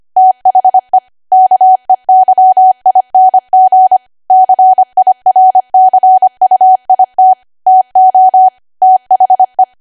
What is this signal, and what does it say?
Morse code, heard as a keyed tone, about 25 wpm, 750 Hz. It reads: THE KEYING CIRCUIT TO THE
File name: Archive / 25 WPM / 25WPM